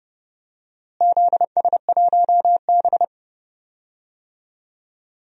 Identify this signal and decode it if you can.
Morse code, heard as a keyed tone, about 30 wpm, 700 Hz. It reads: ZS1B
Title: DXCC-100 Call Signs - 4 Characters - Repeated 2x - 30wpm